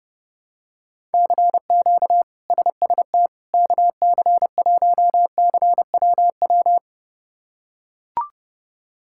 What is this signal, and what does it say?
Morse code, heard as a keyed tone, about 32 wpm, 700 Hz. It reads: CQ SST KC1CWW
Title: K1USN SST QSO Practice - 30wpm